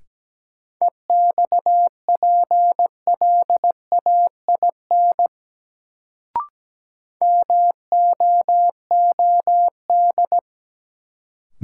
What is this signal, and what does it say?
Morse code, heard as a keyed tone, about 17 wpm, 700 Hz. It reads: EXPLAIN MOOD
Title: Sentences from Top 1100 Words - New Word Drill Alt - 17wpm